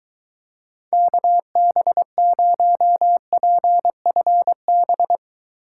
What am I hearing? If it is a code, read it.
Morse code, heard as a keyed tone, about 21 wpm, 700 Hz. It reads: KB0PFB